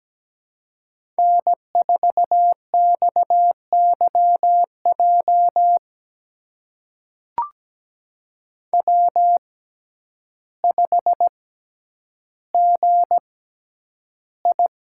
Morse code, heard as a keyed tone, about 17 wpm, spaced as for 9 wpm, 700 Hz. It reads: N4XYJ W5GI
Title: USA Call Signs - 6x Character Spacing - 17wpm